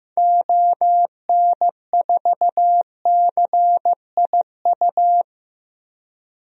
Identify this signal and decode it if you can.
Morse code, heard as a keyed tone, about 14 wpm, 700 Hz. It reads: ON4CIU
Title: DXCC-100 Call Signs - 6 Characters - 15wpm